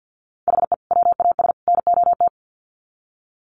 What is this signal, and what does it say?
Morse code, heard as a keyed tone, about 50 wpm, 700 Hz. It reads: HE WAS NOT
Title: Sets of 3 Words 50wpm